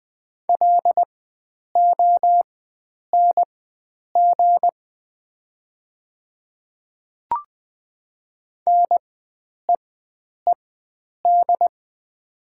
Morse code, heard as a keyed tone, about 20 wpm, spaced as for 9 wpm, 700 Hz. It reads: LONG NEED
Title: 4x Character Spacing - Top 500 Words - 20wpm